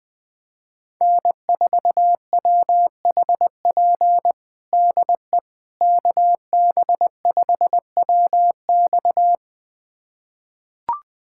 Morse code, heard as a keyed tone, about 20 wpm, 700 Hz. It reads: N4WHP DE KB5WX
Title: USA Callsign DE Callsign - 5 Characters - 20wpm